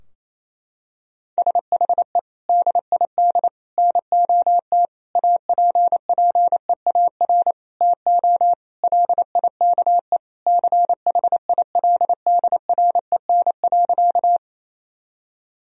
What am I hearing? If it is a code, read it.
Morse code, heard as a keyed tone, about 28 wpm, 700 Hz. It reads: SHE DID NOT APPEAR TO LIKE CHILDREN.
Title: Sentences from Top 1000 Words - Review Part 2 - 28wpm